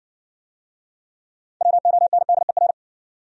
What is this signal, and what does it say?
Morse code, heard as a keyed tone, about 55 wpm, 700 Hz. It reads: WONDER